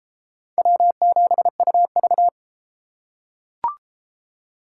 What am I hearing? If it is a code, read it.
Morse code, heard as a keyed tone, about 30 wpm, 700 Hz. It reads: W7UV